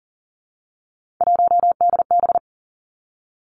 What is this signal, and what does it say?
Morse code, heard as a keyed tone, about 38 wpm, 700 Hz. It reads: 1DB
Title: Three Challenging Letter-Number Combinations 40wpm